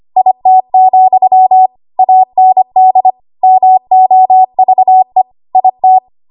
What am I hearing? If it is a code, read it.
Morse code, heard as a keyed tone, about 25 wpm, 750 Hz. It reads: IT, AND MOVE IT
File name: Archive / 25 WPM / 25WPM